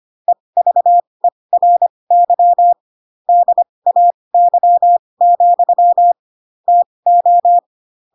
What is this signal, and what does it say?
Morse code, heard as a keyed tone, about 25 wpm, spaced as for 20 wpm, 700 Hz. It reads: EVERY DAY, TO
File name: Pinocchio / Pinocchio ch8 15 25 ik1hge